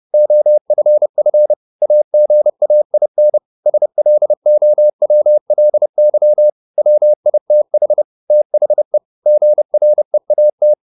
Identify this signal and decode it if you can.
Morse code, heard as a keyed tone, about 30 wpm, 600 Hz. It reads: OFF AGAIN SLOWLY WITH THE GREAT